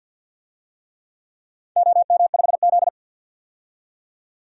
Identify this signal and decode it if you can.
Morse code, heard as a keyed tone, about 49 wpm, 700 Hz. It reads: OM5Z
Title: DXCC-100 Call Signs - 4 Characters - 50wpm